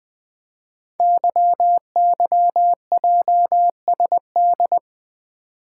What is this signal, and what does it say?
Morse code, heard as a keyed tone, about 20 wpm, 700 Hz. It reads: YYJSD
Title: Code Groups - 5 Letters - 20wpm